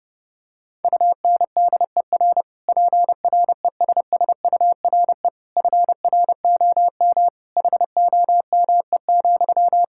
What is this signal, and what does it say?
Morse code, heard as a keyed tone, about 30 wpm, 700 Hz. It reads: UNDER PRESSURE FROM HOME,
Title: Sentences from Top 1000 Words - Review Part 6 - 30wpm